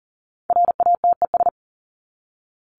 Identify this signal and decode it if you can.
Morse code, heard as a keyed tone, about 40 wpm, 700 Hz. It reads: RATES
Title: ES:S Suffix - Top 100 Words with Base Words 40wpm